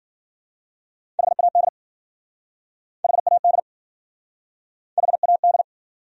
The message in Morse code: HAD HAD HAD

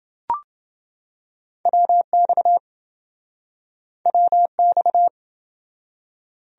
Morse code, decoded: WX WX

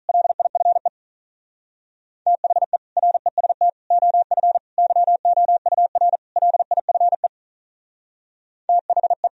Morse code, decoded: LIFE THE REST OF YOUR LIFE THE

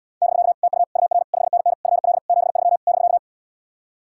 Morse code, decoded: <BK> TU UR 5NN FL 73 <BK>

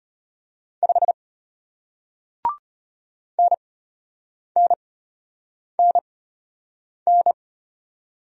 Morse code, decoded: 5 NNNN

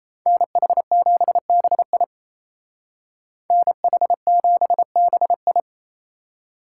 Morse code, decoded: NH7BI NH7BI

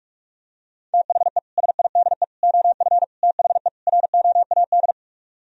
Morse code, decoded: THE SIDE OF THE ROAD

